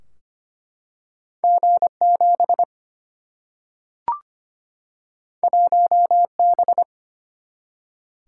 Morse code, heard as a keyed tone, about 25 wpm, 700 Hz. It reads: G 7 1 B
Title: Single-Letter Single-Number 25wpm